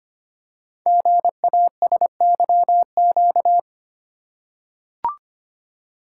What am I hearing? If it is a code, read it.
Morse code, heard as a keyed tone, about 25 wpm, 700 Hz. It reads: GASYQ
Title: Code Groups - 5 Letter:Numbers - 25wpm